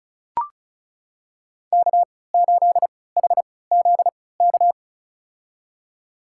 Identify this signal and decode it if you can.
Morse code, heard as a keyed tone, about 35 wpm, spaced as for 20 wpm, 700 Hz. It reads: K8HZK
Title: USA Call Signs - 5 Characters - 3x Character Spacing - 35wpm